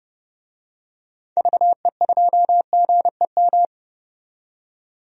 Morse code, heard as a keyed tone, about 30 wpm, 700 Hz. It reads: VE2GEM
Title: DXCC-100 Call Signs - 6 Characters - 30wpm